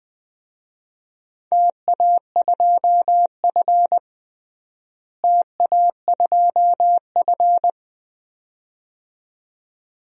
Morse code, heard as a keyed tone, about 21 wpm, 700 Hz. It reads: TA2F TA2F
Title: DXCC-100 Call Signs - 4 Characters - Repeated 2x - 20wpm